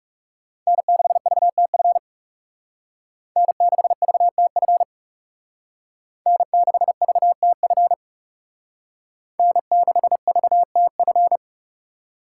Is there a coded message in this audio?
Yes: Morse code from 0.7 to 11.4 s, heard as a keyed tone, about 38 wpm, 700 Hz. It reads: N6VTF N6VTF N6VTF N6VTF